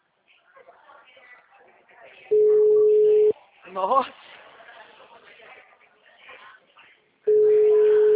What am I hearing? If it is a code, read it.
Ringback tone (Europe, 425 Hz), with speech.